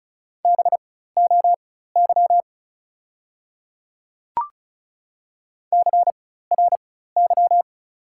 Morse code, heard as a keyed tone, about 36 wpm, spaced as for 16 wpm, 690 Hz. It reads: BOY CRY
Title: 4x Character Spacing - Top 500 Words - 35wpm